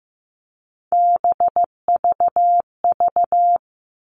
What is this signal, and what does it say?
Morse code, heard as a keyed tone, about 15 wpm, 700 Hz. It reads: BVV
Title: All Cut Numbers - 3-Digit 15wpm